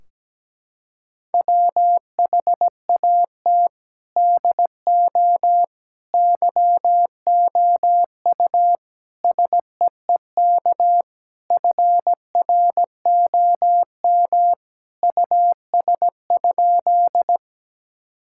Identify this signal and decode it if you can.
Morse code, heard as a keyed tone, about 17 wpm, 700 Hz. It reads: WHAT DO YOU SEEK FROM US?
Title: Sentences from Top 1600 Words 17wpm